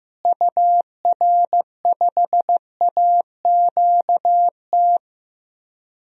Morse code, heard as a keyed tone, about 15 wpm, 700 Hz. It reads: UR5AQT
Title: DXCC-100 Call Signs - 6 Characters - Repeated 2x - 15wpm